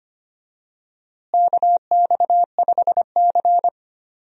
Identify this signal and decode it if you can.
Morse code, heard as a keyed tone, about 25 wpm, 700 Hz. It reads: KX5C
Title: USA Call Signs - 4 Characters - Repeated 2x - 25wpm